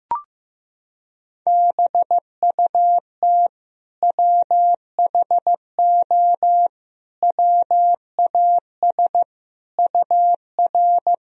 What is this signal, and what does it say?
Morse code, heard as a keyed tone, about 15 wpm, 700 Hz. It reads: BUT WHO WAS UR
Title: Sentences from Top 200 Words and Abbreviations 15wpm